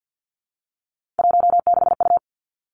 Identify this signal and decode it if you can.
Morse code, heard as a keyed tone, about 50 wpm, 700 Hz. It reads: 16U
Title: Three Challenging Letter-Number Combinations 50wpm